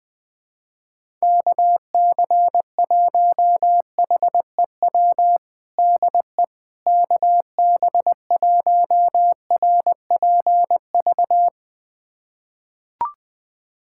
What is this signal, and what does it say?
Morse code, heard as a keyed tone, about 21 wpm, 700 Hz. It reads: KC1HEW DE KB1RPV
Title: USA Callsign DE Callsign - 6 Characters - 20wpm